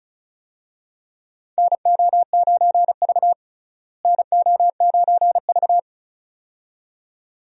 Morse code, heard as a keyed tone, about 34 wpm, 700 Hz. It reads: NO9V NO9V
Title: USA Call Signs - 4 Characters - Repeated 2x - 35wpm